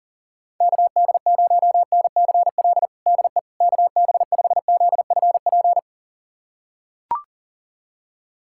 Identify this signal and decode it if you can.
Morse code, heard as a keyed tone, about 40 wpm, 700 Hz. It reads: KD0NCL DE KB5ZFP
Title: USA Callsign DE Callsign - 6 Characters - 40wpm